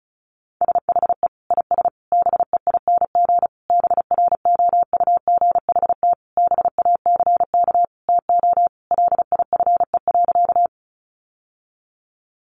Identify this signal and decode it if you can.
Morse code, heard as a keyed tone, about 34 wpm, 700 Hz. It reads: SHE IS BEING BROUGHT BACK TO LIFE.
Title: Sentences from Top 600 Words 35wpm